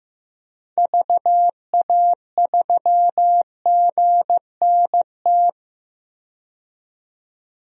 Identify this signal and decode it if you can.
Morse code, heard as a keyed tone, about 15 wpm, 700 Hz. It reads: VA3GNT
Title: DXCC-100 Call Signs - 6 Characters - 15wpm